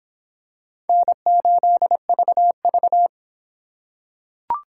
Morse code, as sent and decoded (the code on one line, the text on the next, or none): -. ---.. ...- ...-
N8VV